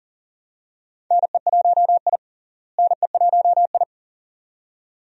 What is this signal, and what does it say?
Morse code, heard as a keyed tone, about 39 wpm, 700 Hz. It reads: NE1I NE1I
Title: USA Call Signs - 4 Characters - Repeated 2x - 40wpm